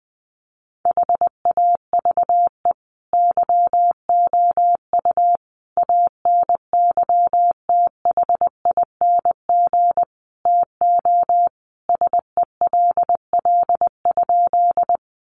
Morse code, heard as a keyed tone, about 20 wpm, 700 Hz. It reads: HAVE YOU ANYTHING TO SELL?
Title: Sentences from Top 1100 Words 20wpm